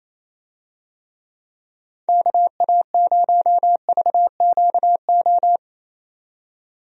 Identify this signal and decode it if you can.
Morse code, heard as a keyed tone, about 28 wpm, 700 Hz. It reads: KA0VQO
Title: USA Call Signs - 6 Characters - 28wpm